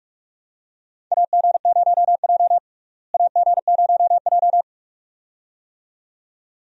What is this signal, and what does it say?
Morse code, heard as a keyed tone, about 41 wpm, 700 Hz. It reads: AG0J AG0J
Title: USA Call Signs - 4 Characters - Repeated 2x - 45wpm